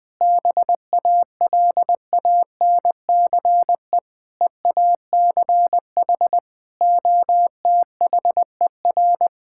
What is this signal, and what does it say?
Morse code, heard as a keyed tone, about 21 wpm, 700 Hz. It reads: BALANCE EACH OTHER
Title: Sentences from Top 1000 Words - Review Part 6 - 20wpm